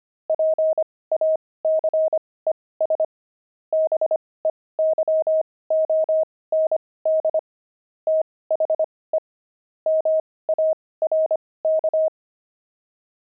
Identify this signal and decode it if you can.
Morse code, heard as a keyed tone, about 25 wpm, spaced as for 18 wpm, 630 Hz. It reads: PACES BEYOND THE MARK
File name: pangram-fs-18-cs-25-javelins